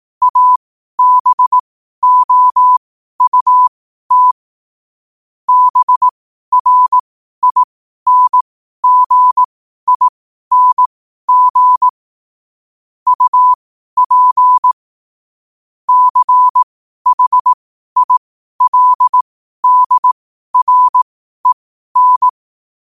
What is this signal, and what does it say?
Morse code, heard as a keyed tone, about 18 wpm, spaced as for 12 wpm, 1000 Hz. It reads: ABOUT BRINGING UP CHILDREN
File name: qotdfiles / 12wpm